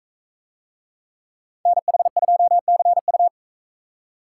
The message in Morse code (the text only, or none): NH2CU